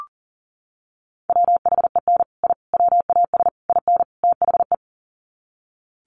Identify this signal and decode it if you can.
Morse code, heard as a keyed tone, about 37 wpm, 700 Hz. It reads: WHEN I WAS IN THE
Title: Sets of 5 Words Encore2 40wpm